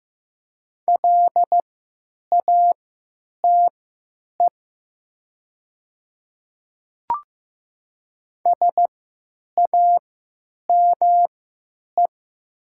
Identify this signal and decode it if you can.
Morse code, heard as a keyed tone, about 15 wpm, spaced as for 9 wpm, 700 Hz. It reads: LATE SAME
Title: 3x Character Spacing - Top 500 Words - 15wpm